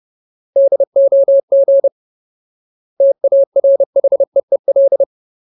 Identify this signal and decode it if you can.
Morse code, heard as a keyed tone, about 30 wpm, 550 Hz. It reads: DOG TARHEEL
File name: Ultimate Word List / 30WPM Ultimate